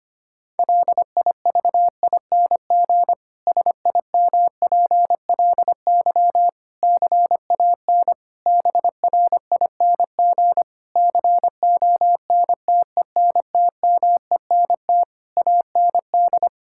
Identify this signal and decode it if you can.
Morse code, heard as a keyed tone, about 25 wpm, 700 Hz. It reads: LIVING SIMPLY CAN BRING CONTENTMENT AND
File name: practice / 25wpm / Difficult Sentences with Hints - 25wpm